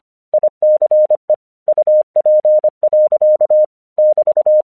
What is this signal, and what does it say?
Morse code, heard as a keyed tone, about 25 wpm, 610 Hz. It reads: ICE UP. <BT>
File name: 1 English 25wpm / AQC16